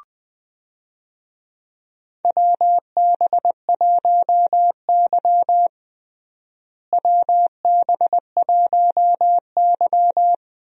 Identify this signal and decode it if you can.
Morse code, heard as a keyed tone, about 20 wpm, 700 Hz. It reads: WB1Y WB1Y
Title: USA Call Signs - 4 Characters - Repeated 2x - 20wpm